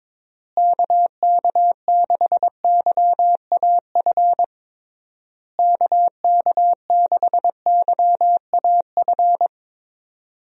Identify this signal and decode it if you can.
Morse code, heard as a keyed tone, about 22 wpm, 700 Hz. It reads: KK6YAF KK6YAF
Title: USA Call Signs - 6 Characters - Repeated 3x - 22wpm